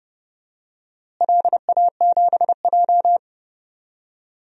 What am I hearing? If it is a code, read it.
Morse code, heard as a keyed tone, about 32 wpm, 700 Hz. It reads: LA7J
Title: DXCC-100 Call Signs - 4 Characters - Repeated 3x - 30wpm